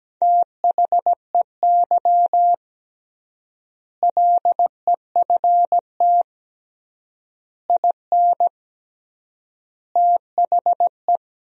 Morse code, heard as a keyed tone, about 17 wpm, 700 Hz. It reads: THEY LEFT IN THE
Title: 3x Word-Spacing - Sentences from Top 400 Words - 17wpm